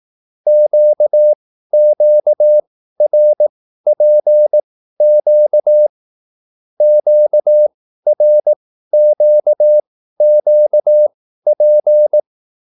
Morse code, heard as a keyed tone, about 18 wpm, spaced as for 13 wpm, 600 Hz. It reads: QQRPQ QRQQP